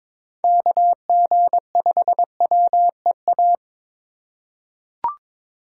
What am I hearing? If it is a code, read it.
Morse code, heard as a keyed tone, about 22 wpm, 700 Hz. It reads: KG5WEA
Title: USA Call Signs - 6 Characters - 2x Character Spacing - 22wpm